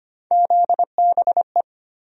Morse code, heard as a keyed tone, about 25 wpm, 700 Hz. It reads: ZBE
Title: USA Call Sign Postfixes - 3 Letters 25wpm